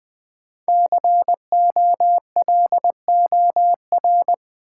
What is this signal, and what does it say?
Morse code, heard as a keyed tone, about 20 wpm, 700 Hz. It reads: COLOR
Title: Sentences from Top 1000 Words - Review Part 2 - New Word Drill - 20wpm